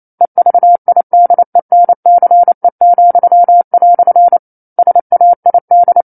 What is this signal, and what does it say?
Morse code, heard as a keyed tone, about 28 wpm, 700 Hz. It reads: EVIDENCE," SAID
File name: 09-Chapter-09 28 WPM